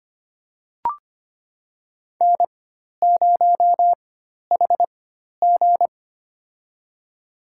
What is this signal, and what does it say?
Morse code, heard as a keyed tone, about 25 wpm, spaced as for 12 wpm, 700 Hz. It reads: N0HG
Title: USA Call Signs - 4 Characters - 4x Character Spacing - 25wpm